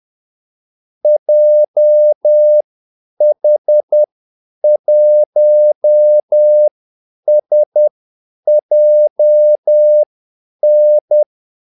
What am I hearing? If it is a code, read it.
Morse code, heard as a keyed tone, about 10 wpm, spaced as for 8 wpm, 600 Hz. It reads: JH1SJN
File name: Callsigns-10-8wpm